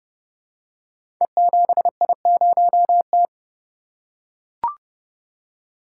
Morse code, heard as a keyed tone, about 30 wpm, 700 Hz. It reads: E7I0T
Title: Code Groups - 5 Letter:Numbers - 30wpm